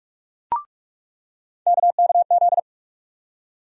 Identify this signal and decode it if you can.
Morse code, heard as a keyed tone, about 46 wpm, 700 Hz. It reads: KKZ
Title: USA Call Sign Postfixes - 3 Letters 45wpm